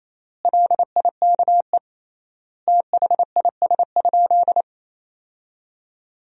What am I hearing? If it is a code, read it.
Morse code, heard as a keyed tone, about 28 wpm, 700 Hz. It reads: LIKE THIS?
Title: 3x Word-Spacing - Sentences from Top 200 Words - 28wpm